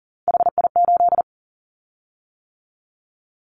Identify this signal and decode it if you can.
Morse code, heard as a keyed tone, about 40 wpm, 700 Hz. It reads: HI8
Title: Callsign Prefixes for Top 100 DXCC - 40wpm